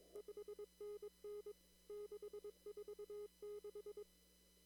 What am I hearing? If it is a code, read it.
Morse code, heard as a keyed tone, about 22 wpm, 410 Hz. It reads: 5NN 646
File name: CW / kt5j wpx cw-op-k5pi